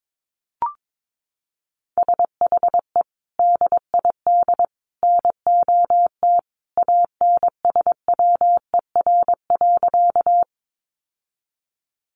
Morse code, heard as a keyed tone, about 22 wpm, 700 Hz. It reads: SHE DID NOT ANSWER.
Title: Sentences from Top 600 Words 22wpm